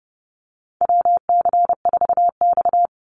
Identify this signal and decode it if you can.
Morse code, heard as a keyed tone, about 31 wpm, 700 Hz. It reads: WC4X
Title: Call Signs - 30wpm